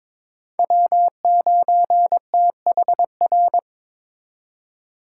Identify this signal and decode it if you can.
Morse code, heard as a keyed tone, about 26 wpm, 700 Hz. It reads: W9THR
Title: Speed Racing - USA 1x3 Call Signs 22wpm